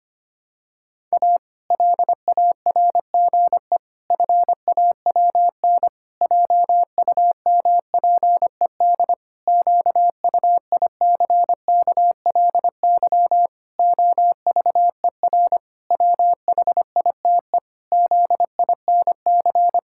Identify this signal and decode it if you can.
Morse code, heard as a keyed tone, about 25 wpm, 700 Hz. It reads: A LARGE FAWN JUMPED QUICKLY OVER WHITE ZINC
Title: Pangram Challenge v2 - 25wpm